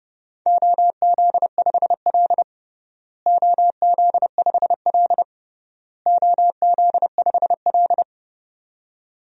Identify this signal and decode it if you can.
Morse code, heard as a keyed tone, about 30 wpm, 700 Hz. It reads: OZ5L OZ5L OZ5L
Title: DXCC-100 Call Signs - 4 Characters - Repeated 3x - 30wpm